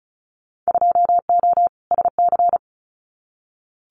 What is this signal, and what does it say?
Morse code, heard as a keyed tone, about 36 wpm, 700 Hz. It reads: 2O SC
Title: Winter Field Day Exchanges 35wpm